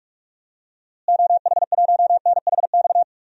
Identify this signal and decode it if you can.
Morse code, heard as a keyed tone, about 46 wpm, 700 Hz. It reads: OH1NHX
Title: DXCC-100 Call Signs - 6 Characters - Repeated 3x - 45wpm